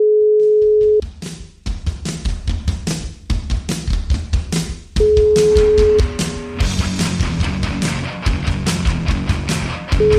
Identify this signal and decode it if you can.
Ringback tone (Europe, 425 Hz).